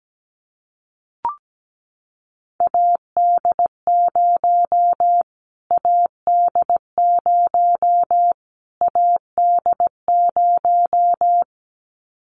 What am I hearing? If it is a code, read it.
Morse code, heard as a keyed tone, about 17 wpm, 700 Hz. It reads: AD0 AD0 AD0
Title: Callsign Prefixes for Top 25 DXCC - Repeated 3x - 17wpm